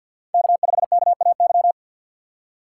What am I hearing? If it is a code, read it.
Morse code, heard as a keyed tone, about 53 wpm, 700 Hz. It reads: K5KAY
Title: USA Call Signs - 5 Characters - 3x Character Spacing - 50wpm